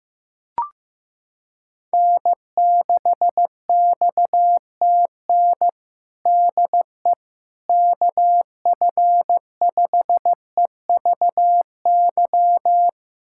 Morse code, heard as a keyed tone, about 15 wpm, 700 Hz. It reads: N6XTN DE KF5EVY
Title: USA Callsign DE Callsign 15wpm